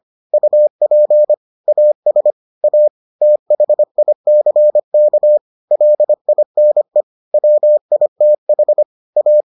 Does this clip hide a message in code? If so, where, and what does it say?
Yes: Morse code from 0.3 to 9.4 s, heard as a keyed tone, about 25 wpm, 600 Hz. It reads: UP AS A THICK LINE WITH A